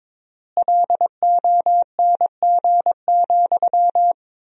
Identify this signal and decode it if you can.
Morse code, heard as a keyed tone, about 22 wpm, 700 Hz. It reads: LONG,
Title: 3x Word-Spacing - Sentences from Top 500 Words - 22wpm